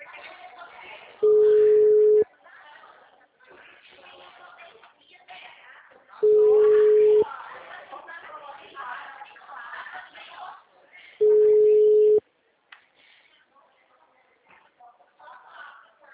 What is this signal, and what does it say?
Ringback tone (Europe, 425 Hz), with speech.